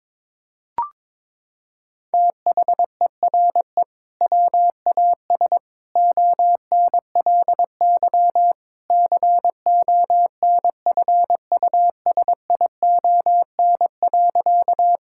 Morse code, heard as a keyed tone, about 21 wpm, 700 Hz. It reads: THERE WAS ONLY CONFUSION.
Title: Sentences from Top 1600 Words 22wpm